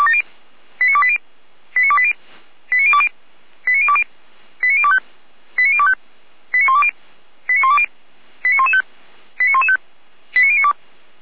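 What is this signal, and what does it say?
Five-tone selective call (ZVEI), heard as quick runs of five tones, tones of about 70 ms. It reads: 89280 89280 89020 89020 89025 89025 89120 89120 89115 89115 89092
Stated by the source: ZVEI five-tone call received by SDR for the Orlamünde fire brigade (Feuerwehr Orlamünde).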